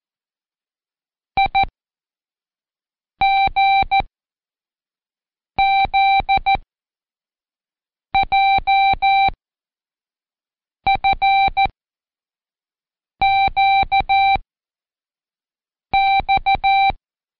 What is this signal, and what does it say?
Morse code, heard as a keyed tone, about 14 wpm, 780 Hz. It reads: I G Z J F Q X
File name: Code Quick / AudioDownld1 / Seg59-Day7-CheckTest 7